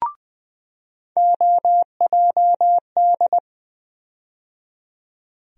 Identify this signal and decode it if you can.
Morse code, heard as a keyed tone, about 20 wpm, 700 Hz. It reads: OJD
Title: USA Call Sign Postfixes - 3 Letters 20wpm